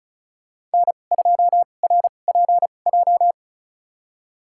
Morse code, heard as a keyed tone, about 34 wpm, spaced as for 25 wpm, 700 Hz. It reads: N2RPJ
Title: USA Call Signs - 5 Characters - 2x Character Spacing - 35wpm